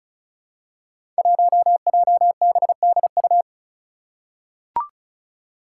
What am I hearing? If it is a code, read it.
Morse code, heard as a keyed tone, about 36 wpm, 700 Hz. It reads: 1JBDU
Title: Five Challenging Letter-Number Combinations 35wpm